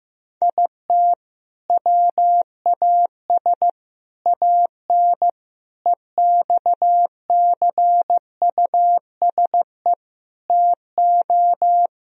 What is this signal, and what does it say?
Morse code, heard as a keyed tone, about 15 wpm, 700 Hz. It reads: IT WAS AN EXCUSE TO
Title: Sentences from Top 1000 Words - Review Part 1 - 15wpm